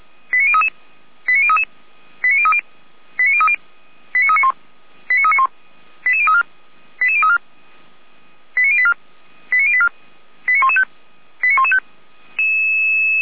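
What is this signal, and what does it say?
Five-tone selective call (ZVEI), heard as quick runs of five tones, tones of about 70 ms. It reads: 89033 89033 89030 89030 89391 89391 88034 88034 89084 89084 89115 89115
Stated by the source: ZVEI five-tone selective call 89376 received on an SDR.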